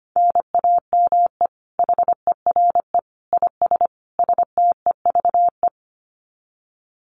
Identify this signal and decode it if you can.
Morse code, heard as a keyed tone, about 25 wpm, 700 Hz. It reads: NAME HERE IS STEVE
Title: Common QSO Phrases 25wpm